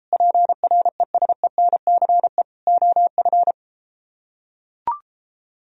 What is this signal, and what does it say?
Morse code, heard as a keyed tone, about 33 wpm, 700 Hz. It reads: PRESENCE OF